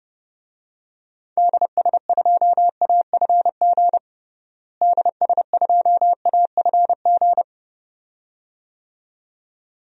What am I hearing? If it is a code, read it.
Morse code, heard as a keyed tone, about 32 wpm, 700 Hz. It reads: DS2AFG DS2AFG
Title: DXCC-100 Call Signs - 6 Characters - Repeated 2x - 30wpm